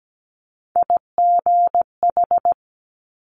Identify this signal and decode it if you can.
Morse code, heard as a keyed tone, about 17 wpm, 700 Hz. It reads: IGH
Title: Multi-letter Phonemes Quick Fire 17wpm